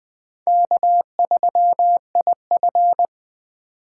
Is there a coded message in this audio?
Yes: Morse code from 0.5 to 3.1 s, heard as a keyed tone, about 20 wpm, 700 Hz. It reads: K3IF